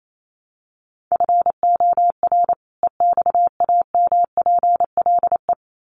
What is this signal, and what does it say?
Morse code, heard as a keyed tone, about 28 wpm, 700 Hz. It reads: FOR EXAMPLE
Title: Sets of 2 Words 28wpm